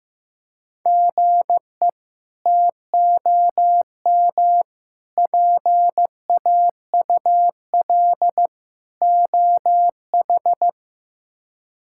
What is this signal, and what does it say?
Morse code, heard as a keyed tone, about 15 wpm, 700 Hz. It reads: GE TOM PAUL OH
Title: K1USN SST QSO Practice - 15wpm